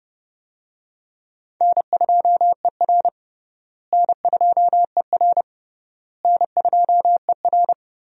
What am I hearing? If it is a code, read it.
Morse code, heard as a keyed tone, about 30 wpm, 700 Hz. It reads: N2ER N2ER N2ER
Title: USA Call Signs - 4 Characters - Repeated 3x - 30wpm